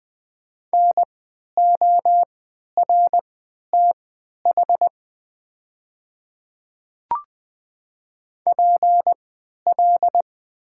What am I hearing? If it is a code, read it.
Morse code, heard as a keyed tone, about 20 wpm, spaced as for 11 wpm, 690 Hz. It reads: NORTH PL
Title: 3x Character Spacing - Top 300 Words - 20wpm